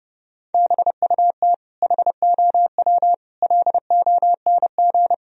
Morse code, heard as a keyed tone, about 30 wpm, 700 Hz. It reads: BUT HOW LONG
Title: Sentences from Top 1000 Words - Review Part 3 - 30wpm